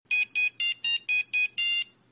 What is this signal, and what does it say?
There is no code.